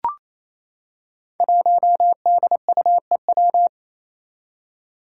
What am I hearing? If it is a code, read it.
Morse code, heard as a keyed tone, about 28 wpm, 700 Hz. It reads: 1DUEW
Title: Five Challenging Letter-Number Combinations 28wpm